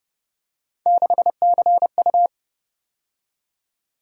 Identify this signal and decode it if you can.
Morse code, heard as a keyed tone, about 30 wpm, 700 Hz. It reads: 6CU